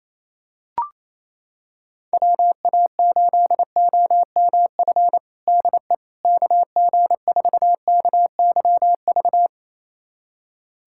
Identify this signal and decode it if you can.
Morse code, heard as a keyed tone, about 27 wpm, 700 Hz. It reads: WA8OMF DE KG4KYV
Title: USA Callsign DE Callsign - 6 Characters - 28wpm